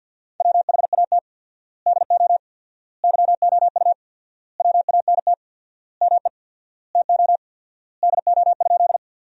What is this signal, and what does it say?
Morse code, heard as a keyed tone, about 50 wpm, 700 Hz. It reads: WHAT DO YOU WANT ME TO DO?
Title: 4x Word-Spacing - Sentences from Top 200 Words - 50wpm